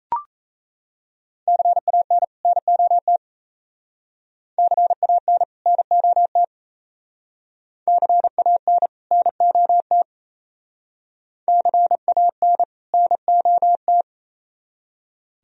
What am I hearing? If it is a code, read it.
Morse code, heard as a keyed tone, about 34 wpm, 700 Hz. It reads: CAN NOT CAN NOT CAN NOT CAN NOT
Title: Speed-Racing - Sets of 2 Words 28wpm